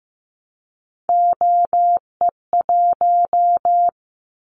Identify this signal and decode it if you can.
Morse code, heard as a keyed tone, about 15 wpm, 700 Hz. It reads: OE1
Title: Callsign Prefixes for Top 25 DXCC - 15wpm